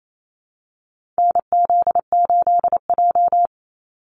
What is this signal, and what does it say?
Morse code, heard as a keyed tone, about 28 wpm, 700 Hz. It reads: NZ8J
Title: Call Signs Repeated 2x - 28wpm